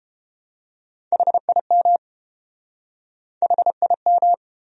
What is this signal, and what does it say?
Morse code, heard as a keyed tone, about 30 wpm, 700 Hz. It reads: HIM HIM